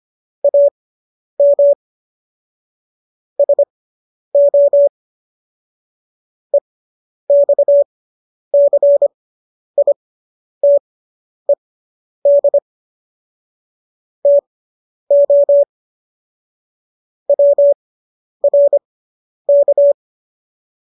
Morse code, heard as a keyed tone, about 25 wpm, spaced as for 10 wpm, 570 Hz. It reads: AM SO EXCITED TO WRK